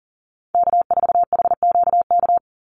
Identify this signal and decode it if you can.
Morse code, heard as a keyed tone, about 40 wpm, 700 Hz. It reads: K4HQK
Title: Call Signs - 40wpm